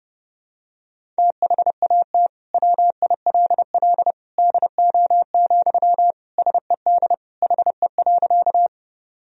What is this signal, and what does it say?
Morse code, heard as a keyed tone, about 29 wpm, 700 Hz. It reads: THAT WILL DO, SED HE.
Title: Sentences from Top 100 Words and Abbreviations 30wpm